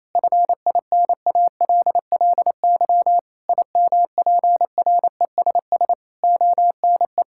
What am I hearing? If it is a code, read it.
Morse code, heard as a keyed tone, about 28 wpm, 700 Hz. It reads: FINALLY IMPRESS ONE
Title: Sentences from Top 1600 Words 28wpm